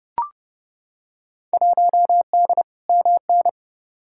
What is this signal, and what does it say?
Morse code, heard as a keyed tone, about 30 wpm, 700 Hz. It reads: 1D MN
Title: Field Day Exchanges - 30wpm